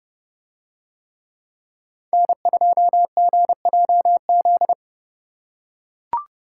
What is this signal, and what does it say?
Morse code, heard as a keyed tone, about 30 wpm, 700 Hz. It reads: N2GJZ